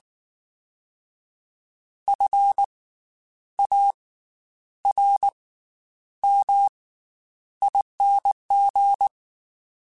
Morse code, heard as a keyed tone, about 19 wpm, spaced as for 9 wpm, 800 Hz. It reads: FARMING
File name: Special INGs / Special INGs-mp3s / Slow ing @18